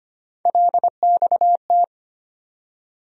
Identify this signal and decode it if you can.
Morse code, heard as a keyed tone, about 25 wpm, 700 Hz. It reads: LXT
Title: USA Call Sign Postfixes - 3 Letters 25wpm